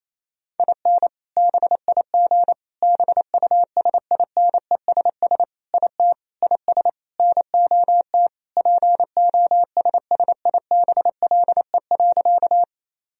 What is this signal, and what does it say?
Morse code, heard as a keyed tone, about 28 wpm, 700 Hz. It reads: IN BIG BUSINESS IT IS NOT POSSIBLE.
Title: Sentences from Top 1000 Words - Review Part 1 - 28wpm